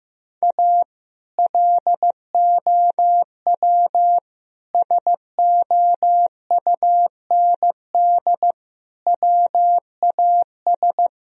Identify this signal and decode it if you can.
Morse code, heard as a keyed tone, about 15 wpm, 700 Hz. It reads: A LOW SOUND WAS
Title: Sentences from Top 1000 Words - Review Part 1 - 15wpm